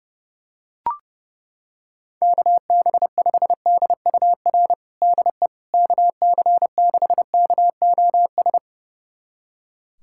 Morse code, heard as a keyed tone, about 29 wpm, 700 Hz. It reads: KB5DUR DE KC6KOS
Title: USA Callsign DE Callsign - 6 Characters - 30wpm